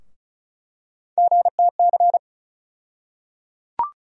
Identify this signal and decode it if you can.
Morse code, heard as a keyed tone, about 36 wpm, 700 Hz. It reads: GTC
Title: USA Call Sign Postfixes - 3 Letters 35wpm